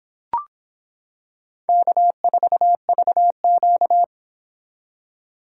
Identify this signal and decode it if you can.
Morse code, heard as a keyed tone, about 26 wpm, 700 Hz. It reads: K4VQ